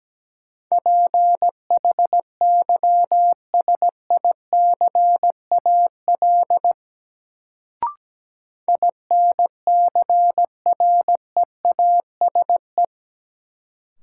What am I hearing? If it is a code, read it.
Morse code, heard as a keyed tone, about 17 wpm, 700 Hz. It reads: PHYSICAL INCREASE
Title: Sentences from Top 1000 Words - Review Part 4 - New Word Drill - 17wpm